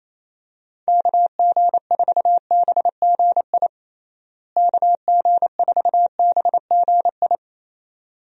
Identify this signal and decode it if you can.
Morse code, heard as a keyed tone, about 27 wpm, 700 Hz. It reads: KG4BGI KG4BGI
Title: USA Call Signs - 6 Characters - Repeated 2x - 28wpm